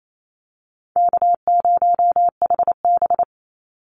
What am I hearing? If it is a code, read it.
Morse code, heard as a keyed tone, about 28 wpm, 700 Hz. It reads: K0HB
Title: Call Signs - 28wpm